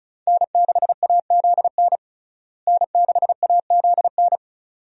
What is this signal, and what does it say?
Morse code, heard as a keyed tone, about 34 wpm, 700 Hz. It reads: N6AZN N6AZN
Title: USA Call Signs - 5 Characters - Repeated 2x - 35wpm